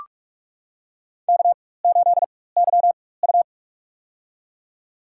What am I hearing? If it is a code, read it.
Morse code, heard as a keyed tone, about 41 wpm, spaced as for 20 wpm, 700 Hz. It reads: K8YU
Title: USA Call Signs - 4 Characters - 4x Character Spacing - 45wpm